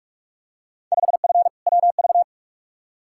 Morse code, heard as a keyed tone, about 42 wpm, 700 Hz. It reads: 5F WV
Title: Field Day Exchanges Encore - 45wpm